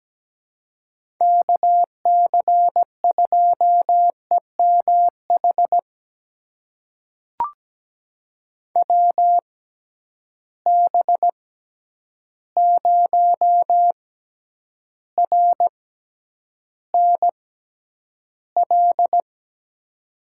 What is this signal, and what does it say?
Morse code, heard as a keyed tone, about 17 wpm, spaced as for 9 wpm, 700 Hz. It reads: KC2EMH WB0RNL
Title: USA Call Signs - 6 Characters - 6x Character Spacing - 17wpm